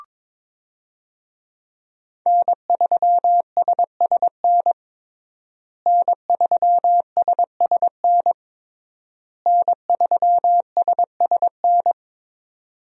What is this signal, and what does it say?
Morse code, heard as a keyed tone, about 22 wpm, 700 Hz. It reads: N3SSN N3SSN N3SSN
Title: USA Call Signs - 5 Characters - Repeated 3x - 22wpm